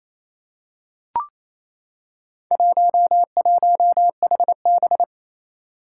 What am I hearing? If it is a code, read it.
Morse code, heard as a keyed tone, about 29 wpm, 700 Hz. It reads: 11HB
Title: Four Challenging Letter-Number Combinations 28wpm